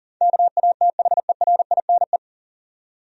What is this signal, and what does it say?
Morse code, heard as a keyed tone, about 40 wpm, 700 Hz. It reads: KATHERINE
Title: USA Common Names - 40wpm